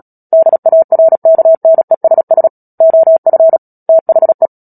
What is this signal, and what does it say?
Morse code, heard as a keyed tone, about 37 wpm, 650 Hz. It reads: DARKNESS OF THE
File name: Book Two / 10-Chapter 10 - Epilogue - 36 WPM